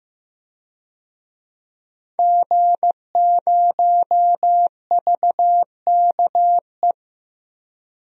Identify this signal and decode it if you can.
Morse code, heard as a keyed tone, about 15 wpm, 700 Hz. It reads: G0VKE
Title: DXCC-100 Call Signs - 5 Characters - Repeated 3x - 15wpm